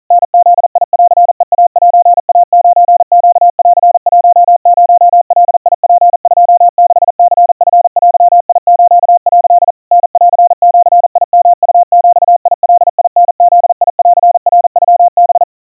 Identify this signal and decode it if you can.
Morse code (Japanese Wabun code), heard as a keyed tone, about 40 wpm, 700 Hz. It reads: タブンヘイセイ9ネン10ガツ26ニチデス」 タンジヨウビガタブンナノハ
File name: kawa / Morse sample / kamyu 150 cpm 700 Hz Crispy